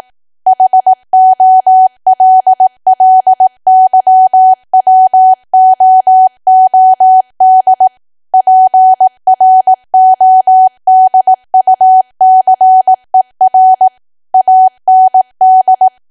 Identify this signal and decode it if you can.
Morse code, heard as a keyed tone, about 18 wpm, 750 Hz. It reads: HOLLYWOOD PRODUCER AND
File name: Archive / 18 WPM / 18WPM